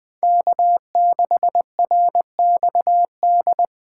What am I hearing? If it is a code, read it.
Morse code, heard as a keyed tone, about 21 wpm, 700 Hz. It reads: K6RXD
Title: Speed Racing - USA 1x3 Call Signs 17wpm